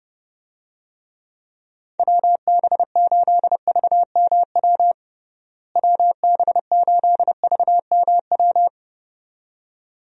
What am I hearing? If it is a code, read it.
Morse code, heard as a keyed tone, about 32 wpm, 700 Hz. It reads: WB8VMW WB8VMW
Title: USA Call Signs - 6 Characters - Repeated 2x - 30wpm